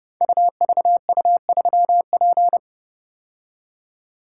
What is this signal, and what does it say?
Morse code, heard as a keyed tone, about 30 wpm, 700 Hz. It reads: UVU3P
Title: Code Groups - 5 Letter:Numbers - 30wpm